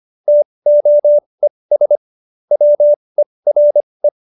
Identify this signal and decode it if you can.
Morse code, heard as a keyed tone, about 25 wpm, spaced as for 20 wpm, 590 Hz. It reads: TOES WERE